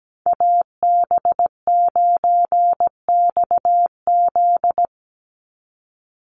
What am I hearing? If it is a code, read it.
Morse code, heard as a keyed tone, about 17 wpm, 700 Hz. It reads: AB9XZ
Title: USA Call Signs 17wpm